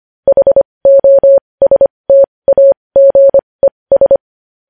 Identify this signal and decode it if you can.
Morse code, heard as a keyed tone, about 25 wpm, spaced as for 20 wpm, 550 Hz. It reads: HOSTAGES
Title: News Headlines at 20